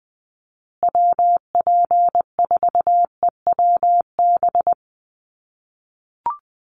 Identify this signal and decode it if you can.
Morse code, heard as a keyed tone, about 20 wpm, 700 Hz. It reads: WP4EWB
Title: USA Call Signs Repeated 2x 20wpm